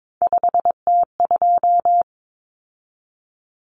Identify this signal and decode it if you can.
Morse code, heard as a keyed tone, about 22 wpm, 700 Hz. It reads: 5T2
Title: Common Cut Numbers - 3-Digit 22wpm